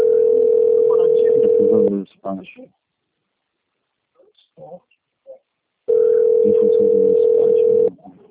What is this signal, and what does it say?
Ringback tone (North America), 440 + 480 Hz, with speech.
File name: out- - - - - .19968